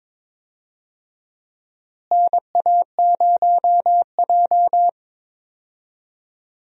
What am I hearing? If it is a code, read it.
Morse code, heard as a keyed tone, about 22 wpm, 700 Hz. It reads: NA0J
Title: USA Call Signs - 4 Characters - Repeated 2x - 22wpm